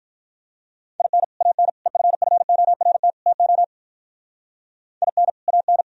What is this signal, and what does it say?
Morse code, heard as a keyed tone, about 43 wpm, 700 Hz. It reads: IN AN EFFORT TO IN AN
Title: Speed-Racing - Sets of 4 Words 35wpm